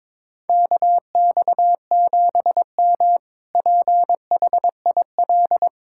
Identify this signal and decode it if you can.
Morse code, heard as a keyed tone, about 21 wpm, 700 Hz. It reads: KX7M PHIL
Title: K1USN SST QSO Practice - 22wpm